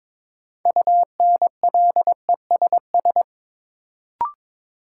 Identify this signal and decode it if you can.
Morse code, heard as a keyed tone, about 22 wpm, 700 Hz. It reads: UNLESS
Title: Sentences from Top 1000 Words - Review Part 5 - New Word Drill - 22wpm